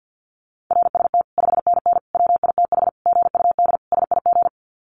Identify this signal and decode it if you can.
Morse code, heard as a keyed tone, about 49 wpm, 700 Hz. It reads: RST 5NN WITH GUD SIG
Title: Common QSO Phrases 50wpm